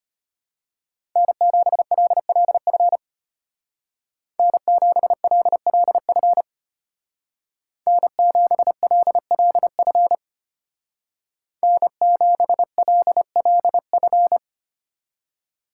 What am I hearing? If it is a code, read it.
Morse code, heard as a keyed tone, about 33 wpm, 700 Hz. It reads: N7LLF N7LLF N7LLF N7LLF